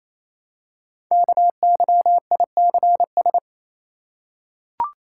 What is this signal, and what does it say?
Morse code, heard as a keyed tone, about 27 wpm, 700 Hz. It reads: KYICS
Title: Code Groups - 5 Letters - 28wpm